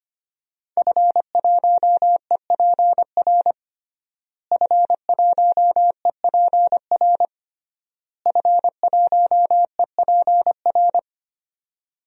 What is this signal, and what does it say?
Morse code, heard as a keyed tone, about 25 wpm, 700 Hz. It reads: F1EPR F1EPR F1EPR
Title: DXCC-100 Call Signs - 5 Characters - Repeated 3x - 25wpm